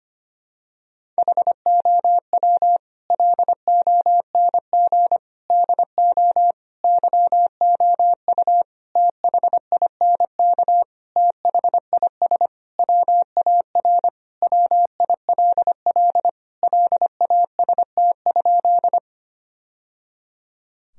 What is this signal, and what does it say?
Morse code, heard as a keyed tone, about 25 wpm, 700 Hz. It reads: HOW LONG DO YOU THINK THIS WAR WILL LAST?
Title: Sentences from Top 1000 Words - Review Part 2 - 25wpm